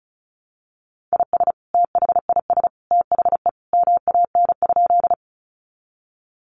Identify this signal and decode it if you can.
Morse code, heard as a keyed tone, about 34 wpm, 700 Hz. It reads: IS THIS THE MAN?
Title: Sentences from Top 200 Words 35wpm